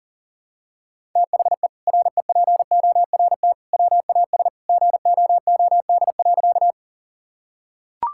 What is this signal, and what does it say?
Morse code, heard as a keyed tone, about 40 wpm, 700 Hz. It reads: THE REPORT WAS GOOD.
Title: Sentences from Top 1000 Words - Review Part 6 - 40wpm